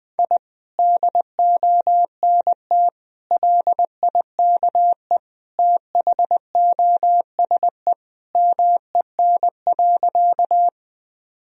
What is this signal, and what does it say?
Morse code, heard as a keyed tone, about 19 wpm, 700 Hz. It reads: I DONT LIKE THOSE MEN.
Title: Sentences from Top 1000 Words - Review Part 1 - 20wpm